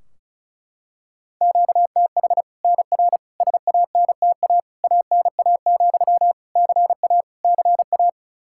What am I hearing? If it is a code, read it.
Morse code, heard as a keyed tone, about 34 wpm, 700 Hz. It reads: QTH NR SANTA ANA, CA CA
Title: Practice QSO - Exchange 1 - Beginning - 35wpm